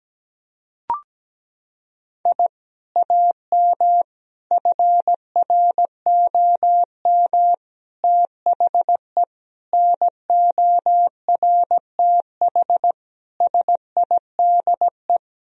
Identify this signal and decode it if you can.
Morse code, heard as a keyed tone, about 17 wpm, 700 Hz. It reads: I AM FROM THE NORTH SIDE
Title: Sentences from Top 1000 Words - Review Part 1 - 17wpm